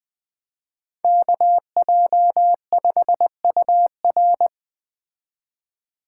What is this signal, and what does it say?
Morse code, heard as a keyed tone, about 21 wpm, 700 Hz. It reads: KJ5UR
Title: Speed Racing - USA 2x2 Call Signs 17wpm